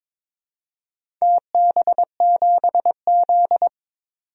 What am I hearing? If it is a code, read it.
Morse code, heard as a keyed tone, about 22 wpm, 700 Hz. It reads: TB7Z
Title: DXCC-100 Call Signs - 4 Characters - 22wpm